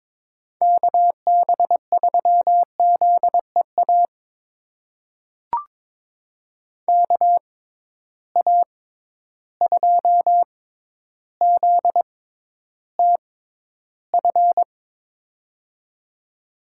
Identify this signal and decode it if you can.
Morse code, heard as a keyed tone, about 21 wpm, spaced as for 11 wpm, 700 Hz. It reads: KB3ZEA KA2ZTF
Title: USA Call Signs - 6 Characters - 6x Character Spacing - 22wpm